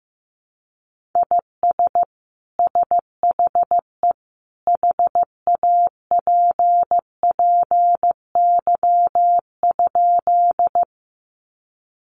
Morse code, heard as a keyed tone, about 15 wpm, 700 Hz. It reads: IS SHE HAPPY?
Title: Sentences from Top 1000 Words 15wpm